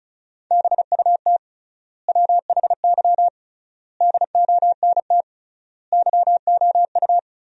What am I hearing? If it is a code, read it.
Morse code, heard as a keyed tone, about 36 wpm, 700 Hz. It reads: BUT WHY DONT YOU
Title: 3x Word-Spacing - Sentences from Top 300 Words - 35wpm